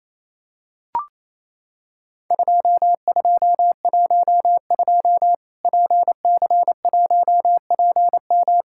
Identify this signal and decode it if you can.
Morse code, heard as a keyed tone, about 28 wpm, 700 Hz. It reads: 2212 PC1PM
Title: Worked All Europe DX Contest - QTC Exchange - 28wpm